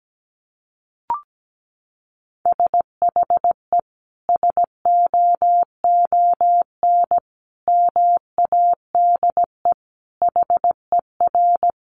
Morse code, heard as a keyed tone, about 17 wpm, 700 Hz. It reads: SHE SOON MADE HER
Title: Sentences from Top 1100 Words 17wpm